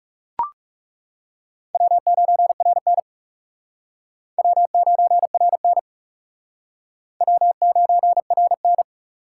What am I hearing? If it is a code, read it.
Morse code, heard as a keyed tone, about 38 wpm, 700 Hz. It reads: W9RN W9RN W9RN